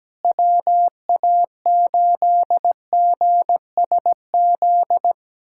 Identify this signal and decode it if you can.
Morse code, heard as a keyed tone, about 17 wpm, 700 Hz. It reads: WA8GSZ
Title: USA Callsign DE Callsign 17wpm